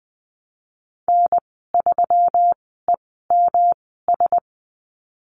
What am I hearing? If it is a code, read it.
Morse code, heard as a keyed tone, about 20 wpm, spaced as for 15 wpm, 700 Hz. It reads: N3EMS
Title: 122 Lesson 20g - Number 3 - Call Signs 20wpm